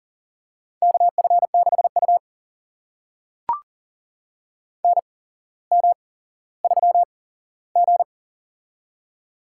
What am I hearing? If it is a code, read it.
Morse code, heard as a keyed tone, about 40 wpm, spaced as for 17 wpm, 700 Hz. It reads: KF6U NM3G
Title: USA Call Signs - 4 Characters - 8x Character Spacing - 40wpm